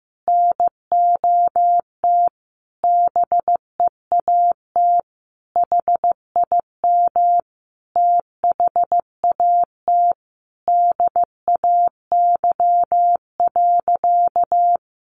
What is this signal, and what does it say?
Morse code, heard as a keyed tone, about 15 wpm, 700 Hz. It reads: NOT BEAT HIM THAT DAY.
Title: Sentences from Top 1300 Words 15wpm